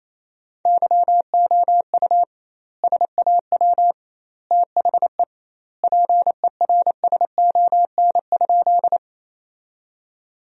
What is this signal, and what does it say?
Morse code, heard as a keyed tone, about 28 wpm, 700 Hz. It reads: YOU SAW THE PERSON?
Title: 2x Word-Spacing - Sentences from Top 400 Words - 28wpm